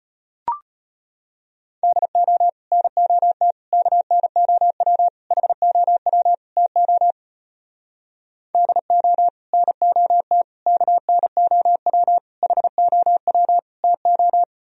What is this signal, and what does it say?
Morse code, heard as a keyed tone, about 38 wpm, 700 Hz. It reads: DO NOT KNOW HOW TO DO NOT KNOW HOW TO